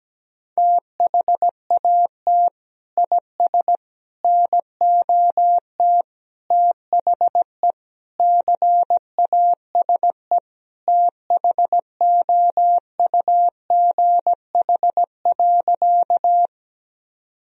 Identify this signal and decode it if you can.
Morse code, heard as a keyed tone, about 17 wpm, 700 Hz. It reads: THAT IS NOT THE CASE THOUGH.
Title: Sentences from Top 1000 Words - Review Part 2 - 17wpm